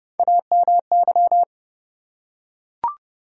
Morse code, heard as a keyed tone, about 30 wpm, 700 Hz. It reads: AMY
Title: USA Common Names - Max 4 Letters - 30wpm